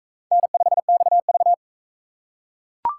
Morse code, heard as a keyed tone, about 39 wpm, 700 Hz. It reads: N5XV